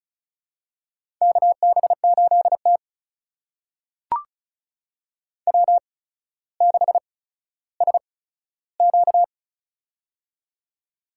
Morse code, heard as a keyed tone, about 36 wpm, spaced as for 15 wpm, 700 Hz. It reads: KB8T W6SQ
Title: USA Call Signs - 4 Characters - 8x Character Spacing - 35wpm